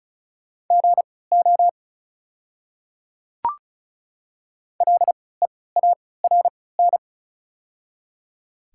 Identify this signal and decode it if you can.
Morse code, heard as a keyed tone, about 36 wpm, spaced as for 20 wpm, 700 Hz. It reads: GO LEARN
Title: 3x Character Spacing - Top 300 Words - 35wpm